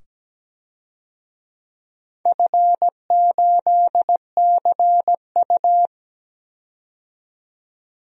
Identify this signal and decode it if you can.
Morse code, heard as a keyed tone, about 17 wpm, 700 Hz. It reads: F8CU
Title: DXCC-100 Call Signs - 4 Characters - 17wpm